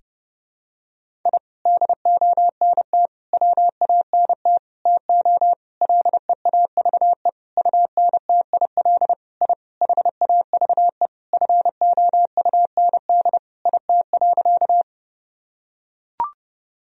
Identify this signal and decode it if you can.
Morse code, heard as a keyed tone, about 30 wpm, 700 Hz. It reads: I DONT WANT TO LEAVE UNTIL I HAVE FOUND IT.
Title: Sentences from Top 1000 Words - Review Part 1 - 30wpm